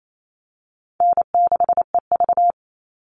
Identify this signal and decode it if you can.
Morse code, heard as a keyed tone, about 28 wpm, 700 Hz. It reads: N6EV
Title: Easy Call Signs - 28wpm